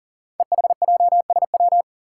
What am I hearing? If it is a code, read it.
Morse code, heard as a keyed tone, about 41 wpm, 690 Hz. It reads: EHJSW
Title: Code Groups - 5 Letters - 40wpm